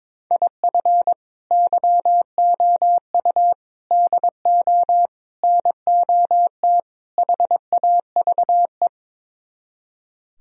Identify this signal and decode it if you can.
Morse code, heard as a keyed tone, about 23 wpm, 700 Hz. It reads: IF YOU DO NOT HAVE
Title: Speed-Racing - Sets of 5 Words 22wpm